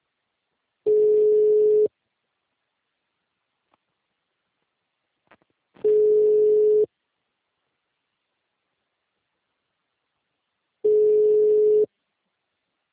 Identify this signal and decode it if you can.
Ringback tone (Europe, 425 Hz).
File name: out- -2019